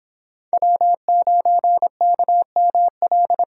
Morse code, heard as a keyed tone, about 22 wpm, 700 Hz. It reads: W9KML